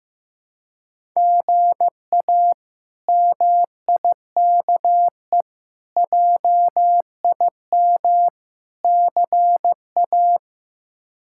Morse code, heard as a keyed tone, about 15 wpm, 700 Hz. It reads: GA MIKE JIM CA
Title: K1USN SST QSO Practice - 15wpm